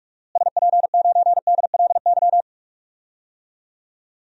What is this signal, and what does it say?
Morse code, heard as a keyed tone, about 46 wpm, 700 Hz. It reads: SP9DLY
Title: DXCC-100 Call Signs - 6 Characters - Repeated 3x - 45wpm